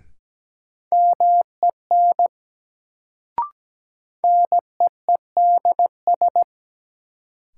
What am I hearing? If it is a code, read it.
Morse code, heard as a keyed tone, about 17 wpm, 700 Hz. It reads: MEN NEEDS
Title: Sets of 3 Words - New Word Drill - 17wpm